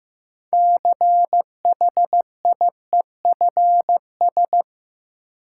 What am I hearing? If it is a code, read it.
Morse code, heard as a keyed tone, about 15 wpm, 700 Hz. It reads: CHIEFS
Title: Sets of 5 Words Encore - New Word Drill - 15wpm